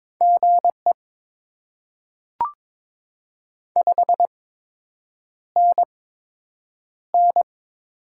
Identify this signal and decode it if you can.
Morse code, heard as a keyed tone, about 22 wpm, spaced as for 8 wpm, 700 Hz. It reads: GE 5NN
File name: Top 20 QSO Elements - 8x Character Spacing - 22wpm